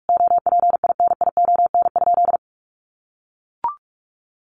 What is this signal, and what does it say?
Morse code, heard as a keyed tone, about 44 wpm, 700 Hz. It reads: OPINION?
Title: Sentences from Top 1200 Words 45wpm